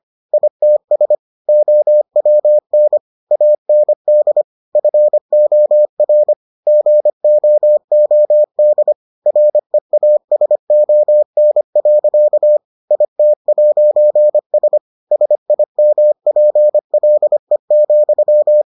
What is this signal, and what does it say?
Morse code, heard as a keyed tone, about 25 wpm, 600 Hz. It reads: ITS OWN AND FOR GOOD REASON. IT'S SIMPLE,